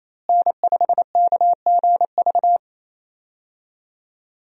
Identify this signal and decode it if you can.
Morse code, heard as a keyed tone, about 28 wpm, 700 Hz. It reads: N5KGV
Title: USA Call Signs - 5 Characters - Repeated 2x - 28wpm